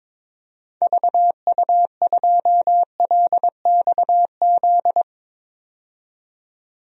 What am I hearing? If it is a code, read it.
Morse code, heard as a keyed tone, about 22 wpm, 700 Hz. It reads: VU2LXZ
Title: DXCC-100 Call Signs - 6 Characters - 22wpm